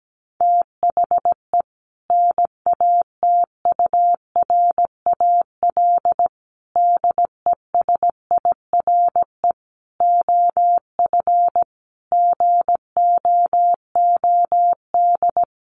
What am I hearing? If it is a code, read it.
Morse code, heard as a keyed tone, about 17 wpm, 700 Hz. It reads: THE NATURAL DESIRE OF GOOD
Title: Sentences from Top 1100 Words 17wpm